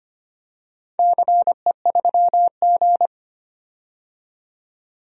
Morse code, heard as a keyed tone, about 25 wpm, 700 Hz. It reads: CE3G
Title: DXCC-100 Call Signs - 4 Characters - 25wpm